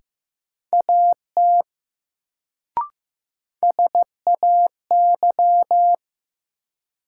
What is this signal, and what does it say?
Morse code, heard as a keyed tone, about 15 wpm, 700 Hz. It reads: AT SAY
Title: Sets of 2 Words - New Word Drill - 15wpm